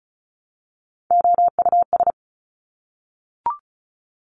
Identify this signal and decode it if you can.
Morse code, heard as a keyed tone, about 36 wpm, 700 Hz. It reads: OUS
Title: Top Three Letter Combinations in Words 35wpm